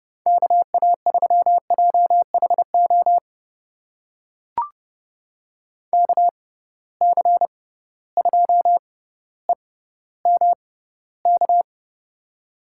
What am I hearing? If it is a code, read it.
Morse code, heard as a keyed tone, about 29 wpm, spaced as for 15 wpm, 700 Hz. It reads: KA3JHO KC2EMK
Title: USA Call Signs - 6 Characters - 6x Character Spacing - 30wpm